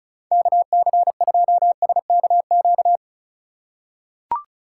Morse code, heard as a keyed tone, about 34 wpm, 690 Hz. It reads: KC2SKQ
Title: USA Call Signs - 4x Character Spacing - 35wpm